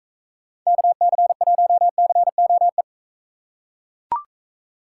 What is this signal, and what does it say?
Morse code, heard as a keyed tone, about 40 wpm, 700 Hz. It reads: KC1COE